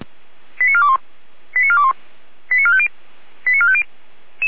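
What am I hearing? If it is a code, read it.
Five-tone selective call (ZVEI), heard as quick runs of five tones, tones of about 80 ms. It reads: 89421 89421 89450 89450
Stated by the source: ZVEI five-tone selective call received via SDR.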